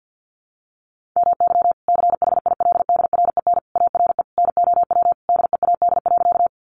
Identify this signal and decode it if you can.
Morse code, heard as a keyed tone, about 50 wpm, 700 Hz. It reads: MY CHILDREN ARE NOW DEAD.
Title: Sentences from Top 1500 Words 50wpm